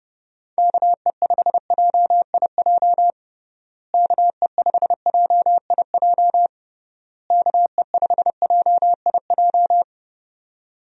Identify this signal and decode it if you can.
Morse code, heard as a keyed tone, about 29 wpm, 700 Hz. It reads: KE5JIJ KE5JIJ KE5JIJ
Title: USA Call Signs - 6 Characters - Repeated 3x - 30wpm